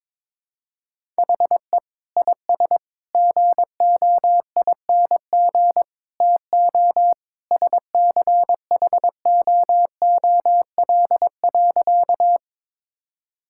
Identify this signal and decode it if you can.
Morse code, heard as a keyed tone, about 21 wpm, 700 Hz. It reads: HE IS GOING TO SCHOOL.
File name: practice / 22wpm / Sentences with 3-Word Phrases - 22wpm